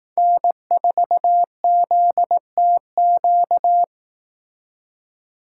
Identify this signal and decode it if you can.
Morse code, heard as a keyed tone, about 18 wpm, 700 Hz. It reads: N4ZTQ
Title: Speed Racing - USA Call Signs 15wpm